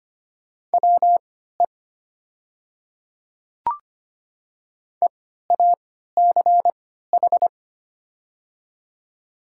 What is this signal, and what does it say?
Morse code, heard as a keyed tone, about 25 wpm, spaced as for 14 wpm, 700 Hz. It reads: WE EACH
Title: 3x Character Spacing - Top 200 Words - 25wpm